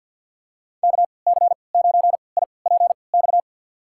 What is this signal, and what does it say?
Morse code, heard as a keyed tone, about 47 wpm, spaced as for 28 wpm, 700 Hz. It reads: KC9IPX
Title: USA Call Signs - 6 Characters - 3x Character Spacing - 50wpm